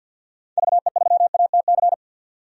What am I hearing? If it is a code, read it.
Morse code, heard as a keyed tone, about 51 wpm, 700 Hz. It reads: VE3ATC
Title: DXCC-100 Call Signs - 6 Characters - 50wpm